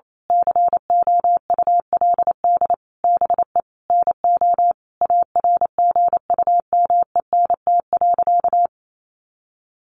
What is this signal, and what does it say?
Morse code, heard as a keyed tone, about 28 wpm, 700 Hz. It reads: COULD BE NO ARGUMENT.
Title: A Sound Of Thunder 28wpm